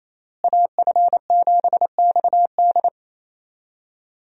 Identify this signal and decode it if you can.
Morse code, heard as a keyed tone, about 28 wpm, 700 Hz. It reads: AF7XD
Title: USA Call Signs - 5 Characters - Repeated 2x - 28wpm